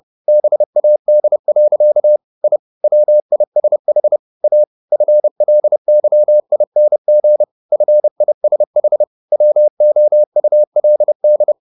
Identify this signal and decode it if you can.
Morse code, heard as a keyed tone, about 30 wpm, 600 Hz. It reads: BAD. I WISH A FLYING FISH WOULD
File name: the old man and the sea / the old man and the sea 1 30wpm